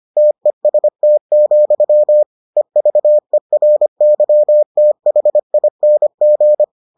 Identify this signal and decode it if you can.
Morse code, heard as a keyed tone, about 25 wpm, 600 Hz. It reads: TEST, EVERYTHING